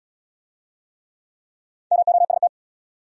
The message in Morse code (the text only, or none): MOST